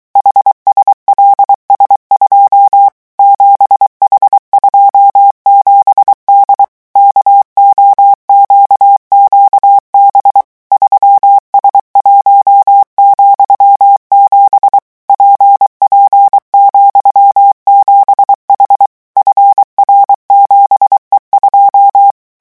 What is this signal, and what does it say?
Morse code, heard as a keyed tone, about 23 wpm, 800 Hz. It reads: HSLS2 7H27D KOQQB 3S1,7 PP,7H FR7E2
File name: morse mp3 / revision13-vitesse 22 mots